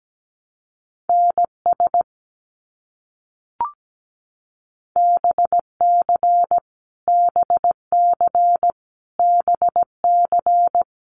Morse code, heard as a keyed tone, about 17 wpm, 700 Hz. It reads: NS BC BC BC
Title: Canadian Provinces and Territories - Repeated 3x - 17wpm